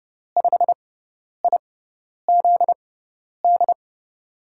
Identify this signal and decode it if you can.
Morse code, heard as a keyed tone, about 30 wpm, spaced as for 10 wpm, 700 Hz. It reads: 5IZD
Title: USA Call Signs - 6 Characters - 6x Character Spacing - 30wpm